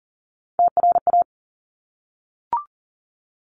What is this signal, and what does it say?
Morse code, heard as a keyed tone, about 39 wpm, 700 Hz. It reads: TRA
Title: Top Three Letter Combinations in Words 40wpm